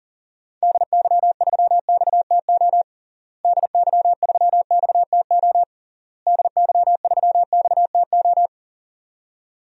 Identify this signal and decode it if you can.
Morse code, heard as a keyed tone, about 40 wpm, 700 Hz. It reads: DY3XTO DY3XTO DY3XTO
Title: DXCC-100 Call Signs - 6 Characters - Repeated 3x - 40wpm